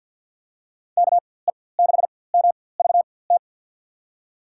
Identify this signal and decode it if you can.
Morse code, heard as a keyed tone, about 53 wpm, spaced as for 24 wpm, 700 Hz. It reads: KE6MVT
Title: USA Call Signs - 6 Characters - 4x Character Spacing - 50wpm